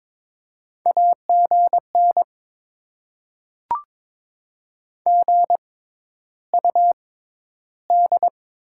Morse code, heard as a keyed tone, about 22 wpm, spaced as for 11 wpm, 700 Hz. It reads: AGN GUD
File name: Top 20 QSO Elements - 6x Character Spacing - 22wpm